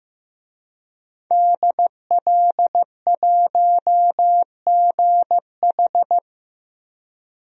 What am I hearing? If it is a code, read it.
Morse code, heard as a keyed tone, about 15 wpm, 700 Hz. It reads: DL1GH
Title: DXCC-100 Call Signs - 5 Characters - 15wpm